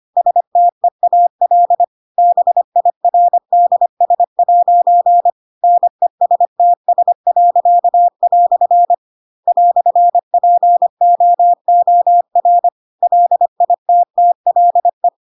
Morse code, heard as a keyed tone, about 25 wpm, 700 Hz. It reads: STEAL BIRDS' NESTS." "POOR LITTLE
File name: Pinocchio / Pinocchio ch4 25 25 ik1hge